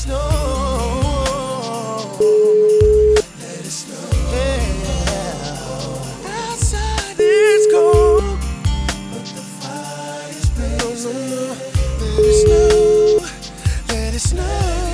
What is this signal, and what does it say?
Ringback tone (Europe, 425 Hz).